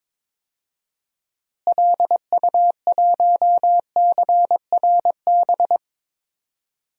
Morse code, heard as a keyed tone, about 22 wpm, 700 Hz. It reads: LU1CRB
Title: DXCC-100 Call Signs - 6 Characters - Repeated 2x - 22wpm